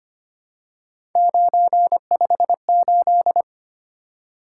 Morse code, heard as a keyed tone, about 25 wpm, 700 Hz. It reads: 958
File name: Three Letter-Number Combinations - 25wpm